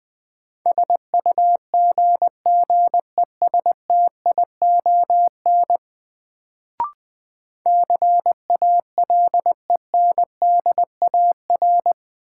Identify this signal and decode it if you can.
Morse code, heard as a keyed tone, about 20 wpm, 700 Hz. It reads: SUGGESTION CALENDAR
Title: Sentences from Top 1600 Words - New Word Drill Alt - 20wpm